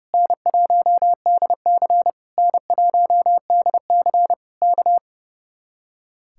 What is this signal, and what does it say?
Morse code, heard as a keyed tone, about 35 wpm, 700 Hz. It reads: N1DC N1DC K
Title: Speed-Racing Common QSO Phrases - 30wpm